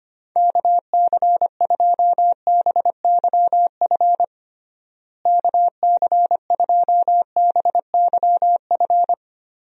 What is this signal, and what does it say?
Morse code, heard as a keyed tone, about 25 wpm, 700 Hz. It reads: KC2BYF KC2BYF
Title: USA Call Signs - 6 Characters - Repeated 3x - 25wpm